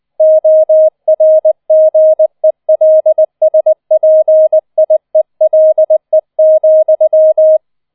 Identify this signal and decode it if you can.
Morse code, heard as a keyed tone, about 20 wpm, 630 Hz. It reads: ORGELSPIELE,